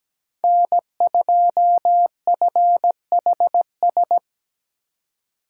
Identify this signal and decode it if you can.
Morse code, heard as a keyed tone, about 17 wpm, 700 Hz. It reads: N2FHS
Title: USA Call Signs - 2x Character Spacing - 17wpm